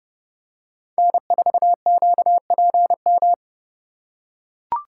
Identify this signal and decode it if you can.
Morse code, heard as a keyed tone, about 30 wpm, 700 Hz. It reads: N4QPM